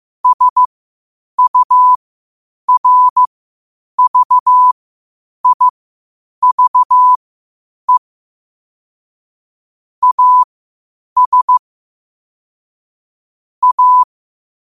Morse code, heard as a keyed tone, about 15 wpm, spaced as for 8 wpm, 1000 Hz. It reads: SURVIVE AS A
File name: qotdfiles / 8wpm